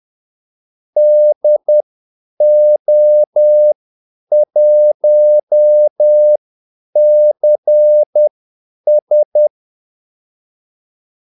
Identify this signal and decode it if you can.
Morse code, heard as a keyed tone, about 10 wpm, spaced as for 8 wpm, 600 Hz. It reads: DO1CS
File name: Callsigns-10-8wpm